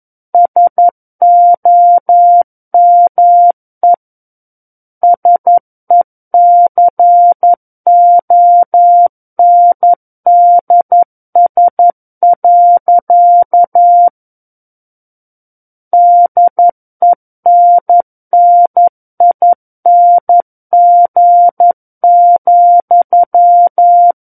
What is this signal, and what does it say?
Morse code, heard as a keyed tone, about 11 wpm, 700 Hz. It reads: SOME SECONDS. DENNING,
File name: Book One / 02-Chapter 2 - 11 WPM